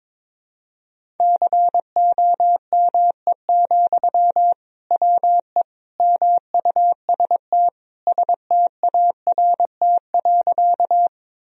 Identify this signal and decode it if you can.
Morse code, heard as a keyed tone, about 22 wpm, 700 Hz. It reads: COME, WE MUST START.
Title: Sentences from Top 1000 Words - Review Part 1 - 22wpm